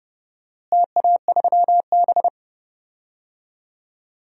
Morse code, heard as a keyed tone, about 30 wpm, 700 Hz. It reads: TA3B
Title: DXCC-100 Call Signs - 4 Characters - Repeated 2x - 30wpm